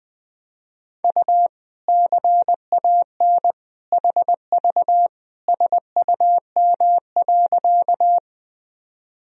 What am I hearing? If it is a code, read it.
Morse code, heard as a keyed tone, about 20 wpm, 700 Hz. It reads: U CAN HV SUM.
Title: Sentences from Top 100 Words and Abbreviations 20wpm